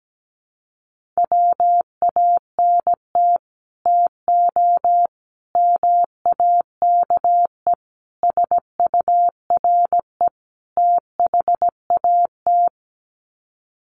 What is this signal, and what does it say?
Morse code, heard as a keyed tone, about 17 wpm, 700 Hz. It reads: WANT TO MAKE SURE THAT
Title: Sets of 5 Words 17wpm